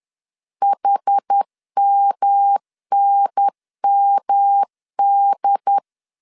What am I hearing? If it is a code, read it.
Morse code, heard as a keyed tone, about 10 wpm, 800 Hz. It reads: HMNMD
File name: CW / DET / 1 3 1500C S B 10wpm 3335